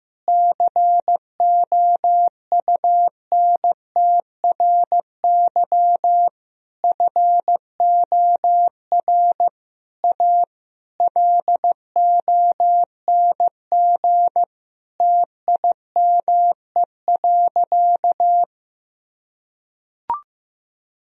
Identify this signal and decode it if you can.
Morse code, heard as a keyed tone, about 15 wpm, 700 Hz. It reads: COUNTRY FOR A LONG TIME.
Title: Sentences from Top 1000 Words - Review Part 1 - 15wpm